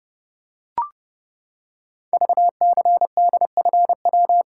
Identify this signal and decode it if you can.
Morse code, heard as a keyed tone, about 30 wpm, 700 Hz. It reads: VCDFW
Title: Code Groups - 5 Letters - 30wpm